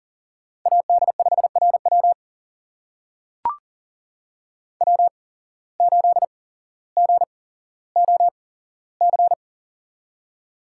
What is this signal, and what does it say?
Morse code, heard as a keyed tone, about 40 wpm, spaced as for 17 wpm, 700 Hz. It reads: AD5RW W8GOC
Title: USA Call Signs - 5 Characters - 8x Character Spacing - 40wpm